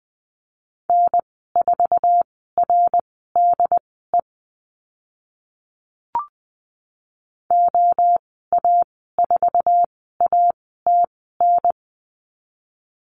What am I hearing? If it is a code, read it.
Morse code, heard as a keyed tone, about 20 wpm, spaced as for 14 wpm, 700 Hz. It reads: N4RDE OA4ATN
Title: 076 Lesson 13g - Letter L - Call Signs 20wpm